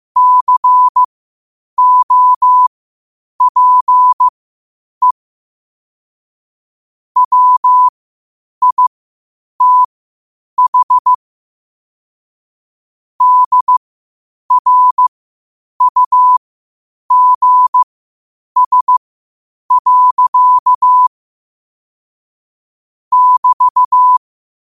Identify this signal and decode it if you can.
Morse code, heard as a keyed tone, about 15 wpm, spaced as for 8 wpm, 1000 Hz. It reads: COPE WITH DRUGS. <BT>
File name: qotdfiles / 8wpm